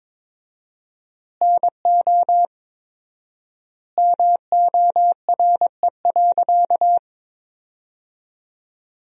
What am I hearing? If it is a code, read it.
Morse code, heard as a keyed tone, about 22 wpm, 700 Hz. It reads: NO MORE.
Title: 4x Word-Spacing - Sentences from Top 200 Words - 22wpm